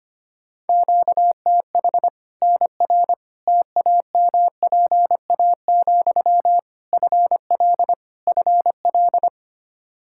Morse code, heard as a keyed tone, about 25 wpm, 700 Hz. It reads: QTH NR TAMPA, FL FL
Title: K1USN SST QSO Practice 25wpm